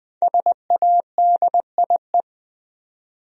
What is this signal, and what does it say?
Morse code, heard as a keyed tone, about 20 wpm, 700 Hz. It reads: SADIE
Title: USA Common Names - Max 5 Letters - 20wpm